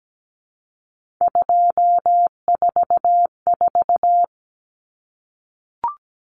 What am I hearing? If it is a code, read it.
Morse code, heard as a keyed tone, about 17 wpm, 700 Hz. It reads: 244
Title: Numbers - 3-Digit 17wpm